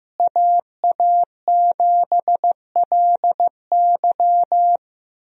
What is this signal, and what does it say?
Morse code, heard as a keyed tone, about 15 wpm, 700 Hz. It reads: AA7LY
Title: USA Call Signs - 5 Characters - Repeated 3x - 15wpm